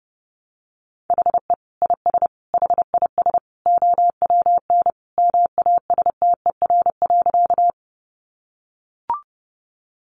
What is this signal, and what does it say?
Morse code, heard as a keyed tone, about 30 wpm, 700 Hz. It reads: HE IS HIS OWN MASTER.
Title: Sentences from Top 1300 Words 30wpm